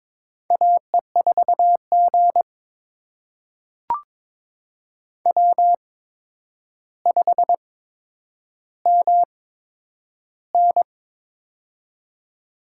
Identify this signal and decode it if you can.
Morse code, heard as a keyed tone, about 21 wpm, spaced as for 9 wpm, 700 Hz. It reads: AE4G W5MN
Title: USA Call Signs - 4 Characters - 8x Character Spacing - 22wpm